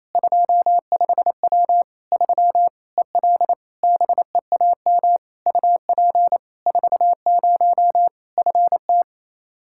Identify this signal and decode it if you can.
Morse code, heard as a keyed tone, about 28 wpm, 700 Hz. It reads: 25W 3 EL BEAM UP 40 FT
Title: Practice QSO - Exchange 2 - Info - 28wpm